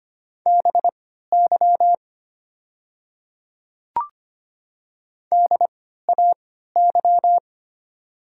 Morse code, heard as a keyed tone, about 25 wpm, spaced as for 14 wpm, 700 Hz. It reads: BY DAY
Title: 3x Character Spacing - Top 100 Words - 25wpm